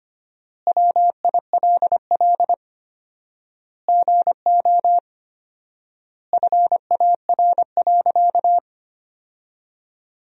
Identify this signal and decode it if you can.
Morse code, heard as a keyed tone, about 25 wpm, 700 Hz. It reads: WILL GO FAR.
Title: 4x Word-Spacing - Sentences from Top 400 Words - 25wpm